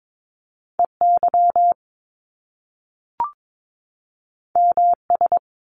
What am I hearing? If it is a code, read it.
Morse code, heard as a keyed tone, about 22 wpm, 700 Hz. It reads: E Y M S
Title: Top Two Letter Combinations in Words 22wpm